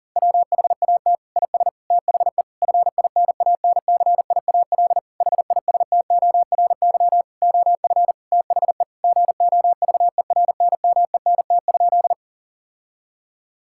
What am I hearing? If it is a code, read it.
Morse code, heard as a keyed tone, about 39 wpm, 700 Hz. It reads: WHAT IS THE FINANCIAL HISTORY OF THE GOVERNMENT?
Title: Sentences from Top 1000 Words - Review Part 5 - 40wpm